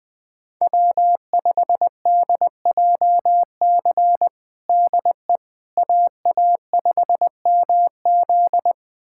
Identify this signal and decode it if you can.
Morse code, heard as a keyed tone, about 20 wpm, 700 Hz. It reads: W5DJC DE AA5MZ
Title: USA Callsign DE Callsign - 5 Characters - 20wpm